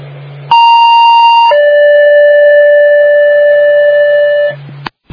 Two-tone page, heard as a one-second tone, then a three-second tone, about 930 then 580 Hz, with noise. It is heard from 0.5 to 4.5 s.
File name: Northville